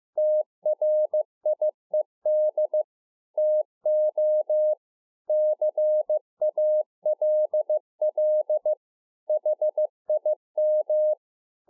Morse code, heard as a keyed tone, about 15 wpm, 610 Hz. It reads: TRIED TO CALL HIM
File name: 2011-04-mp3 / THE 3.11 6 15WPM